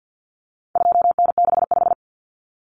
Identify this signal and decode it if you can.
Morse code, heard as a keyed tone, about 49 wpm, 700 Hz. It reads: 2N65
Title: Common Cut Numbers - 4-Digit 50wpm